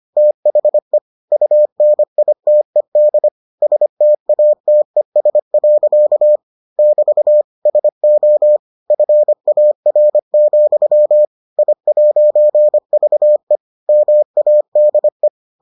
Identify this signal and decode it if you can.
Morse code, heard as a keyed tone, about 25 wpm, 600 Hz. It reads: THE UNITED STATES. <BT> SO FAR, I'VE MADE